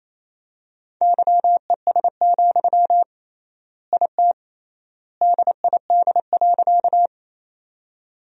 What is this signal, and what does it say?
Morse code, heard as a keyed tone, about 28 wpm, 700 Hz. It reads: YES, IT DID.
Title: 3x Word-Spacing - Sentences from Top 500 Words - 28wpm